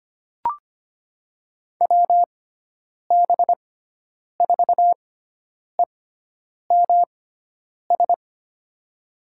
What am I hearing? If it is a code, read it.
Morse code, heard as a keyed tone, about 25 wpm, spaced as for 9 wpm, 700 Hz. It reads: WB4EMS
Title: USA Call Signs - 6x Character Spacing - 25wpm